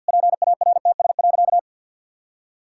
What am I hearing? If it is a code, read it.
Morse code, heard as a keyed tone, about 52 wpm, 700 Hz. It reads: PARTS.
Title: Sentences from Top 1000 Words - Review Part 6 - 50wpm